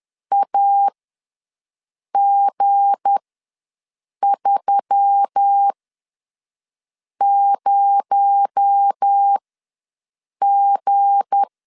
Morse code, heard as a keyed tone, about 10 wpm, spaced as for 5 wpm, 800 Hz. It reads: AG30G
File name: LEZIONI / 7 GF9Z / 7 G ws 10wpm 3843